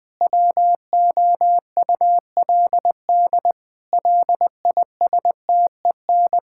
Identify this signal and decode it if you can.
Morse code, heard as a keyed tone, about 20 wpm, 700 Hz. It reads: WOULD LISTEN
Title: Sentences from Top 1600 Words 20wpm